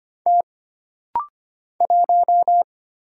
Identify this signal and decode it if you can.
Morse code, heard as a keyed tone, about 25 wpm, 700 Hz. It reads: T 1
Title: Single Letter-Number Alt - Rapid-Fire 25wpm